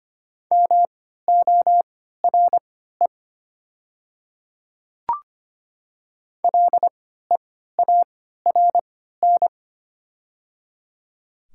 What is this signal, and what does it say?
Morse code, heard as a keyed tone, about 25 wpm, spaced as for 14 wpm, 700 Hz. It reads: MORE LEARN
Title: 3x Character Spacing - Top 300 Words - 25wpm